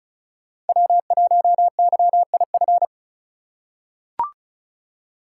Morse code, heard as a keyed tone, about 36 wpm, 700 Hz. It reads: W1YIF
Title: USA Call Signs - 4x Character Spacing - 35wpm